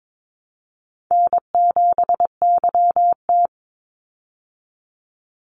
Morse code, heard as a keyed tone, about 22 wpm, 700 Hz. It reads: N7YT
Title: Call Signs Repeated 3x - 22wpm